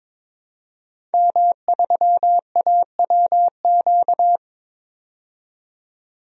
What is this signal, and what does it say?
Morse code, heard as a keyed tone, about 22 wpm, 700 Hz. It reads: M3AWQ
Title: DXCC-100 Call Signs - 5 Characters - 22wpm